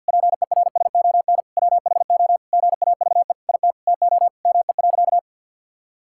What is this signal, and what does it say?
Morse code, heard as a keyed tone, about 50 wpm, 700 Hz. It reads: PERSON WHO GAVE IT TO ME.
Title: Sentences from Top 1600 Words 50wpm